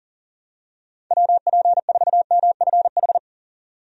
Morse code, heard as a keyed tone, about 37 wpm, 700 Hz. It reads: WP4MFH
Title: USA Call Signs - 6 Characters - 40wpm